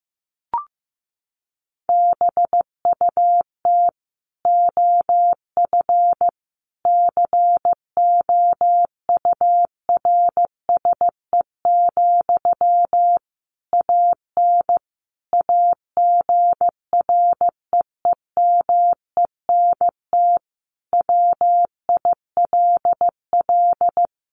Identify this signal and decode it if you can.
Morse code, heard as a keyed tone, about 15 wpm, 700 Hz. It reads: BUT OF COURSE, AN AGREEMENT WILL
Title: Sentences from Top 1500 Words 15wpm